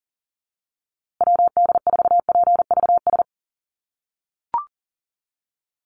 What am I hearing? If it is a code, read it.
Morse code, heard as a keyed tone, about 40 wpm, 700 Hz. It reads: WD4PVS
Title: USA Call Signs Repeated 2x 40wpm